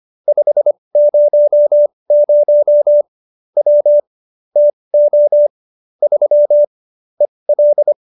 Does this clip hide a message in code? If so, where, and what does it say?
Yes: Morse code from 0.3 to 7.9 s, heard as a keyed tone, about 25 wpm, spaced as for 20 wpm, 590 Hz. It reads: 500 W TO 3 EL